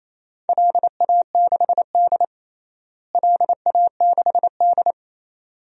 Morse code, heard as a keyed tone, about 28 wpm, 700 Hz. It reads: LA6D LA6D
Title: DXCC-100 Call Signs - 4 Characters - Repeated 2x - 28wpm